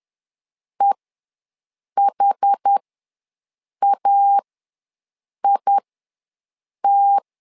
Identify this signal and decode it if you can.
Morse code, heard as a keyed tone, about 10 wpm, spaced as for 6 wpm, 800 Hz. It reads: EHAIT
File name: LEZIONI / 2 UHDN / 2 UHD ws 10wpm 2058